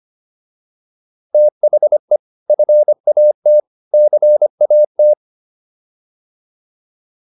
Morse code, heard as a keyed tone, about 25 wpm, 600 Hz. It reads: THE FAT CAT